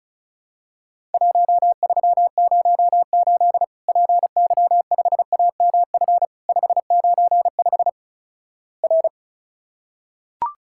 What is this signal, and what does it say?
Morse code, heard as a keyed tone, about 36 wpm, 700 Hz. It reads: 1308 PY5AMF 595 R
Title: Worked All Europe DX Contest - QTC Exchange - 35wpm